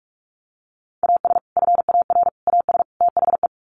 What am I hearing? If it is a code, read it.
Morse code, heard as a keyed tone, about 44 wpm, 700 Hz. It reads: AS FAR AS THE
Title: Sets of 4 Words 45wpm